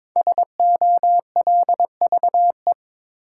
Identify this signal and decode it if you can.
Morse code, heard as a keyed tone, about 22 wpm, 700 Hz. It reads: SOLVE
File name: 5 Letter Words - First 3 Letters Given - 22wpm